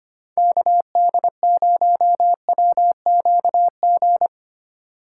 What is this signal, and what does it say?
Morse code, heard as a keyed tone, about 25 wpm, 700 Hz. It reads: KD0WQG
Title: USA Call Signs - 6 Characters - 6x Character Spacing - 25wpm